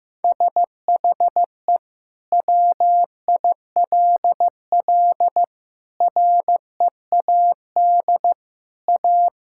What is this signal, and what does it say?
Morse code, heard as a keyed tone, about 16 wpm, 700 Hz. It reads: SHE WILL READ A
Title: Sentences from Top 1000 Words - Review Part 6 - 15wpm